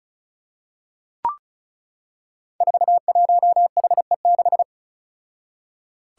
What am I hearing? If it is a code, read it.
Morse code, heard as a keyed tone, about 34 wpm, 700 Hz. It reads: 41HE6
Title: Five Challenging Letter-Number Combinations 35wpm